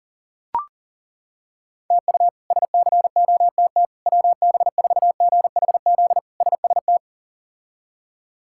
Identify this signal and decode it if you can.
Morse code, heard as a keyed tone, about 40 wpm, 700 Hz. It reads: TU SCOTT WB4GHZ SST
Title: K1USN SST QSO Practice - 40wpm